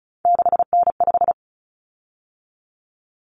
Morse code, heard as a keyed tone, about 34 wpm, 700 Hz. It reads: 6N5
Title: Common Cut Numbers - 3-Digit 35wpm